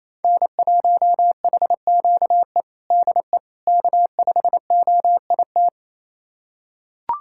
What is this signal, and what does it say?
Morse code, heard as a keyed tone, about 28 wpm, 700 Hz. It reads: N1HQE DE K5OIT
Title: USA Callsign DE Callsign - 5 Characters - 28wpm